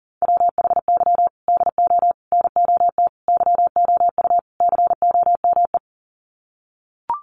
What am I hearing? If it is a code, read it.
Morse code, heard as a keyed tone, about 40 wpm, 700 Hz. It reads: WHY DO NOT YOU COME
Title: Sets of 5 Words Encore2 40wpm